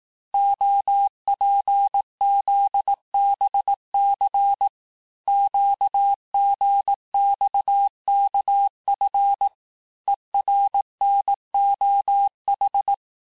Morse code, heard as a keyed tone, about 18 wpm, 800 Hz. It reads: OPZBC QGXKF ERNOH
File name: cours-du-lundi files / cours-1588-texte-chiffre-vitesse-17MPM